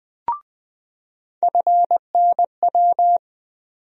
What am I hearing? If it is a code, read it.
Morse code, heard as a keyed tone, about 20 wpm, 700 Hz. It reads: FNW
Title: USA Call Sign Postfixes - 3 Letters 20wpm